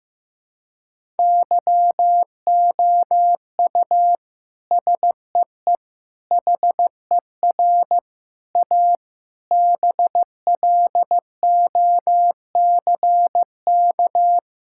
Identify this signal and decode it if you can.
Morse code, heard as a keyed tone, about 15 wpm, 700 Hz. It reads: YOU SEE HER A BLOCK
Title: Sentences from Top 1600 Words 15wpm